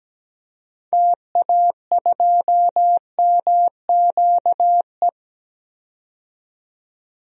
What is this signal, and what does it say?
Morse code, heard as a keyed tone, about 17 wpm, 700 Hz. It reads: TA2MQE
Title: DXCC-100 Call Signs - 6 Characters - 17wpm